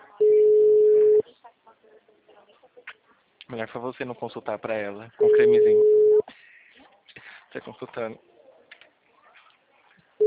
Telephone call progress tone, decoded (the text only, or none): ringback tone